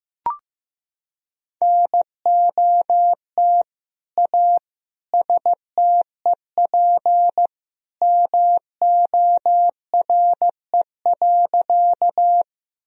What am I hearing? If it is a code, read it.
Morse code, heard as a keyed tone, about 15 wpm, 700 Hz. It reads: NOT A STEP MORE.
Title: Sentences from Top 1000 Words - Review Part 2 - 15wpm